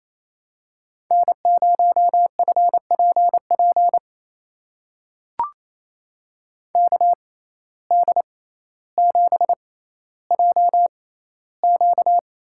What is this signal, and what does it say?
Morse code, heard as a keyed tone, about 28 wpm, spaced as for 14 wpm, 700 Hz. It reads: N0FPP KD7JQ
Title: USA Call Signs - 5 Characters - 6x Character Spacing - 28wpm